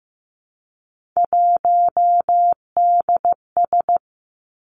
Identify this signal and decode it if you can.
Morse code, heard as a keyed tone, about 15 wpm, 700 Hz. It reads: 1DS
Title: Three Challenging Letter-Number Combinations 15wpm